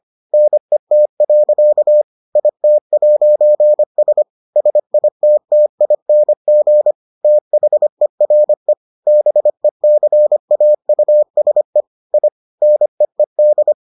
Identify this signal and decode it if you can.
Morse code, heard as a keyed tone, about 25 wpm, 600 Hz. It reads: NET. IT'S SITTING THERE BECAUSE I NEED